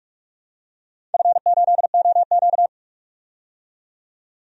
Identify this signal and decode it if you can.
Morse code, heard as a keyed tone, about 42 wpm, 700 Hz. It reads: F8OQ
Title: DXCC-100 Call Signs - 4 Characters - 45wpm